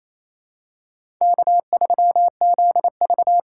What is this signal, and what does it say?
Morse code, heard as a keyed tone, about 27 wpm, 700 Hz. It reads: K3ZV
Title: USA Call Signs - 4 Characters - 4x Character Spacing - 28wpm